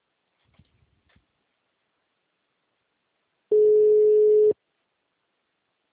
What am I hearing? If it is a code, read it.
Ringback tone (Europe, 425 Hz).